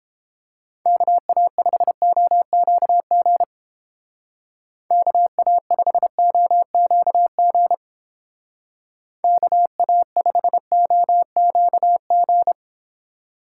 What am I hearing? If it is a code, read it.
Morse code, heard as a keyed tone, about 29 wpm, 700 Hz. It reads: KA5OQG KA5OQG KA5OQG